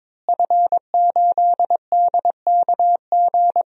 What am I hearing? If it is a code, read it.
Morse code, heard as a keyed tone, about 22 wpm, 700 Hz. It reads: F8DKG
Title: DXCC-100 Call Signs - 5 Characters - Repeated 2x - 22wpm